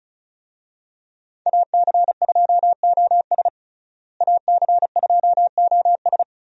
Morse code, heard as a keyed tone, about 36 wpm, 700 Hz. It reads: AC2OS AC2OS
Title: USA Call Signs - 5 Characters - Repeated 2x - 35wpm